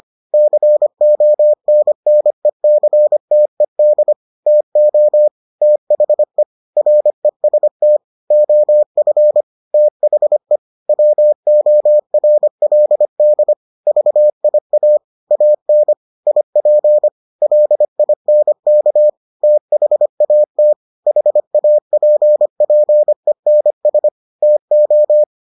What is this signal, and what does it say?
Morse code, heard as a keyed tone, about 25 wpm, 600 Hz. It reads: CONNECTED TO THE REST OF THE WORLD VIA AN IP LINK THAT HAPPENS TO